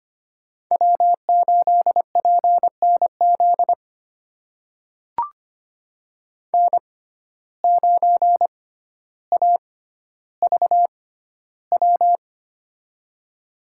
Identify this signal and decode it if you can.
Morse code, heard as a keyed tone, about 25 wpm, spaced as for 13 wpm, 700 Hz. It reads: W8PNZ N9AVW
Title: USA Call Signs - 5 Characters - 6x Character Spacing - 25wpm